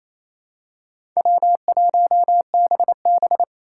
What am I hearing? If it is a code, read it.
Morse code, heard as a keyed tone, about 27 wpm, 700 Hz. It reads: W1BB
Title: USA Call Signs - 4 Characters - 4x Character Spacing - 28wpm